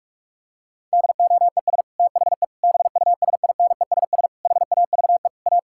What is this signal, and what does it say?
Morse code, heard as a keyed tone, about 43 wpm, 700 Hz. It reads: DOES THE BUSINESS HAVE A
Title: Sentences from Top 1600 Words 45wpm